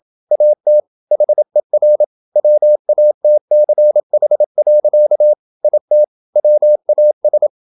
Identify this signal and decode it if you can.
Morse code, heard as a keyed tone, about 27 wpm, 600 Hz. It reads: AT HER WATCH. IT WAS
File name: Never 0027wpm